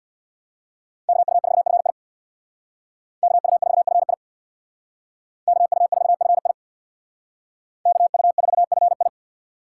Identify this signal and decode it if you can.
Morse code, heard as a keyed tone, about 62 wpm, 700 Hz. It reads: KU4FI KU4FI KU4FI KU4FI
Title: Speed Racing - USA 2x2 Call Signs 50wpm